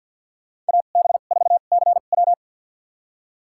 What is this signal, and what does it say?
Morse code, heard as a keyed tone, about 52 wpm, spaced as for 36 wpm, 700 Hz. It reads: AB4CW
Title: USA Call Signs - 5 Characters - 2x Character Spacing - 50wpm